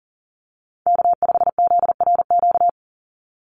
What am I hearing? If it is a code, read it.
Morse code, heard as a keyed tone, about 40 wpm, 700 Hz. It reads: K5ZRQ
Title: USA Call Signs 40wpm